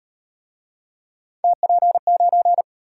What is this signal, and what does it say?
Morse code, heard as a keyed tone, about 32 wpm, 700 Hz. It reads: TP9